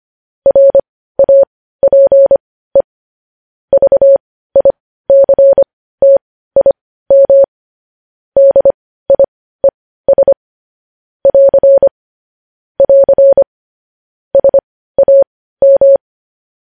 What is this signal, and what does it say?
Morse code, heard as a keyed tone, about 25 wpm, spaced as for 15 wpm, 550 Hz. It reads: RAPE VICTIM DIES <AR> <AR> SAM
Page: news headlines in morse code at 15 wpm